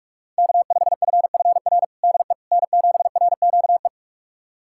Morse code, heard as a keyed tone, about 47 wpm, 700 Hz. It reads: K5FFR DE N7RQE
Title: USA Callsign DE Callsign - 5 Characters - 45wpm